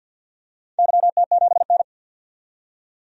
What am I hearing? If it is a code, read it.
Morse code, heard as a keyed tone, about 53 wpm, 700 Hz. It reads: YT7N
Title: DXCC-100 Call Signs - 4 Characters - 50wpm